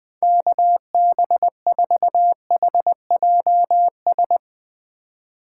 Morse code, heard as a keyed tone, about 21 wpm, 700 Hz. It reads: KB4HJS